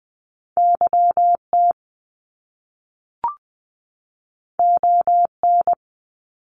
Morse code, heard as a keyed tone, about 19 wpm, 700 Hz. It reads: YT ON
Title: Top Two Letter Combinations in Words 20wpm